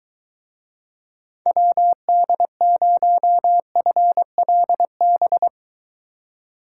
Morse code, heard as a keyed tone, about 23 wpm, 700 Hz. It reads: WD0FLB